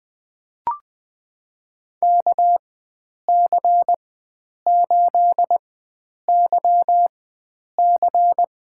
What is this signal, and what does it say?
Morse code, heard as a keyed tone, about 20 wpm, spaced as for 9 wpm, 700 Hz. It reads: KC8YC
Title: USA Call Signs - 4x Character Spacing - 20wpm